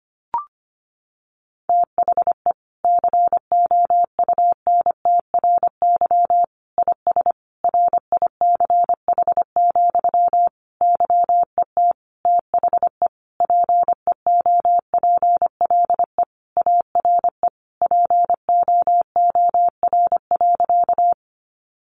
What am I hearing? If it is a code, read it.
Morse code, heard as a keyed tone, about 25 wpm, 700 Hz. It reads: THE COUNTRY IS RICH, YET THE PEOPLE ARE POOR.
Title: Sentences from Top 1100 Words 25wpm